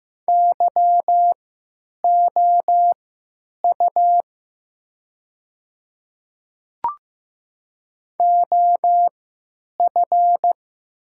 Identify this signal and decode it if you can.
Morse code, heard as a keyed tone, about 15 wpm, spaced as for 9 wpm, 700 Hz. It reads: YOU OF
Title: 3x Character Spacing - Top 100 Words - 15wpm